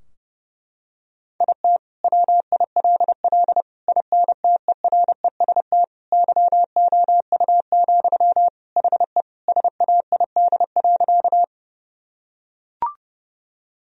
Morse code, heard as a keyed tone, about 31 wpm, 700 Hz. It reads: IT WILL INTEREST YOU, HE SAID.
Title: Sentences from Top 1000 Words - Review Part 2 - 30wpm